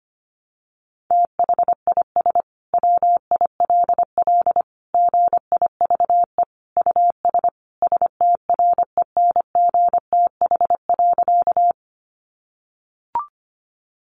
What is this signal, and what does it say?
Morse code, heard as a keyed tone, about 25 wpm, 700 Hz. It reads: THIS WILL GIVE US STRENGTH.
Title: Sentences from Top 1300 Words 25wpm